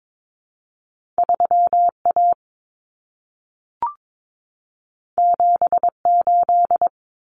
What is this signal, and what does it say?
Morse code, heard as a keyed tone, about 22 wpm, 700 Hz. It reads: 3A 78
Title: Common Cut Numbers - 2-Digit 22wpm